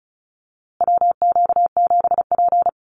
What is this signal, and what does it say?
Morse code, heard as a keyed tone, about 34 wpm, 700 Hz. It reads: WQ7P
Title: USA Call Signs Repeated 2x 35wpm